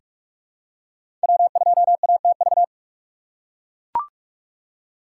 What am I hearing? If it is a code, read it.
Morse code, heard as a keyed tone, about 43 wpm, 700 Hz. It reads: W2ATV
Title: USA Call Signs - 5 Characters - 4x Character Spacing - 45wpm